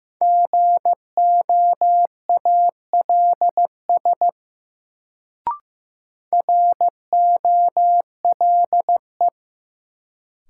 Morse code, heard as a keyed tone, about 16 wpm, 700 Hz. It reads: GOALS ROLE
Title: Sentences from Top 1000 Words - Review Part 6 - New Word Drill - 15wpm